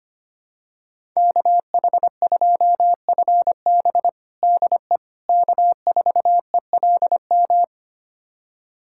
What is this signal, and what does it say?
Morse code, heard as a keyed tone, about 25 wpm, 700 Hz. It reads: KH2FB DE K4ELM
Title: USA Callsign DE Callsign - 5 Characters - 25wpm